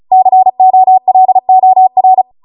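Morse code, heard as a keyed tone, about 36 wpm, 750 Hz. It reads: COLOR
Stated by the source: Amateur radio Morse code practice archive transmission at 35 wpm.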